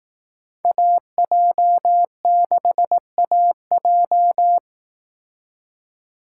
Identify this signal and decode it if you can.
Morse code, heard as a keyed tone, about 18 wpm, 700 Hz. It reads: AJ6AJ
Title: Speed Racing - USA Call Signs 15wpm